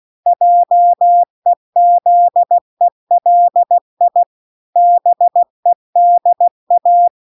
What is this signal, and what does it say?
Morse code, heard as a keyed tone, about 16 wpm, 700 Hz. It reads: JEZELI BEDA